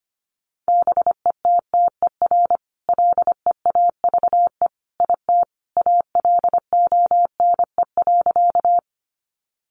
Morse code, heard as a keyed tone, about 25 wpm, 700 Hz. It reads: BETTER LEAVE IT ALONE.
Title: Sentences from Top 1000 Words 25wpm